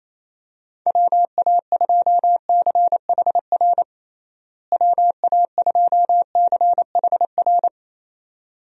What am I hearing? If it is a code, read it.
Morse code, heard as a keyed tone, about 28 wpm, 700 Hz. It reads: WA2CHR WA2CHR
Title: USA Call Signs - 6 Characters - Repeated 2x - 28wpm